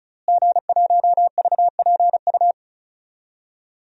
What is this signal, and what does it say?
Morse code, heard as a keyed tone, about 36 wpm, 700 Hz. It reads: G1VPU
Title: DXCC-100 Call Signs - 5 Characters - Repeated 3x - 35wpm